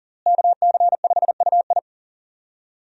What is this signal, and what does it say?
Morse code, heard as a keyed tone, about 40 wpm, 700 Hz. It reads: KC5UI
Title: USA Call Signs - 5 Characters - Repeated 3x - 40wpm